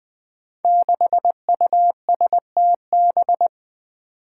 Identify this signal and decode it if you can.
Morse code, heard as a keyed tone, about 20 wpm, 700 Hz. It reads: 6USTB
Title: Five Challenging Letter-Number Combinations 20wpm